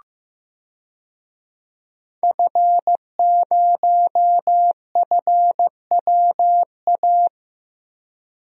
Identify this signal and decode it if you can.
Morse code, heard as a keyed tone, about 15 wpm, 700 Hz. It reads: F0FWA
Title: DXCC-100 Call Signs - 5 Characters - Repeated 2x - 15wpm